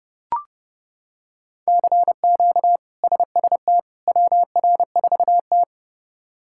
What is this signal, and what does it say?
Morse code, heard as a keyed tone, about 30 wpm, 700 Hz. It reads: CQ SST WR4T
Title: K1USN SST QSO Practice - 30wpm